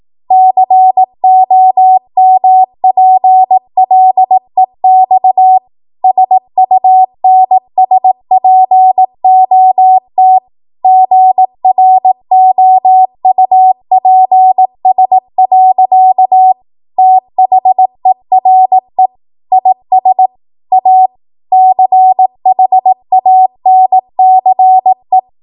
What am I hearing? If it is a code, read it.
Morse code, heard as a keyed tone, about 18 wpm, 750 Hz. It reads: COMPLEX SUNSPOT GROUPS. THERE IS A CHANCE
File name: Morse / 18WPM